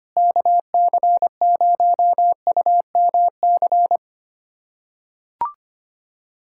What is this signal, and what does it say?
Morse code, heard as a keyed tone, about 25 wpm, 700 Hz. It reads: KC0UMC
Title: USA Callsign DE Callsign - 6 Characters - 25wpm